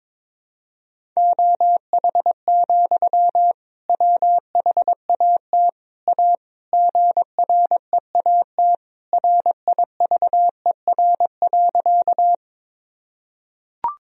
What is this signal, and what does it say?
Morse code, heard as a keyed tone, about 22 wpm, 700 Hz. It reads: OH, WHAT A GREAT RIVER.
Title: Sentences from Top 1000 Words - Review Part 1 - 22wpm